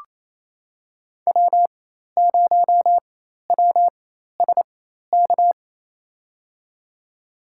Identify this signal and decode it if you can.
Morse code, heard as a keyed tone, about 28 wpm, spaced as for 13 wpm, 700 Hz. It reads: W0WSK
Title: USA Call Signs - 4x Character Spacing - 28wpm